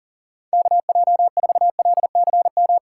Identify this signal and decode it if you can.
Morse code, heard as a keyed tone, about 38 wpm, 690 Hz. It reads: KJ4LCM
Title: USA Call Signs - 6 Characters - 2x Character Spacing - 40wpm